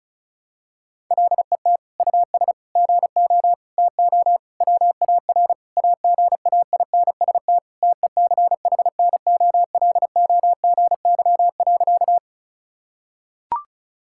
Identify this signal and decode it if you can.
Morse code, heard as a keyed tone, about 36 wpm, 700 Hz. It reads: LET US GO TO WAR AGAINST TECHNOLOGY.
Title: Sentences from Top 1000 Words - Review Part 5 - 35wpm